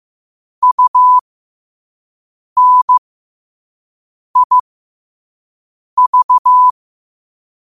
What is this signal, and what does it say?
Morse code, heard as a keyed tone, about 15 wpm, spaced as for 5 wpm, 1000 Hz. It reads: UNIV
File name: qotdfiles / 5wpm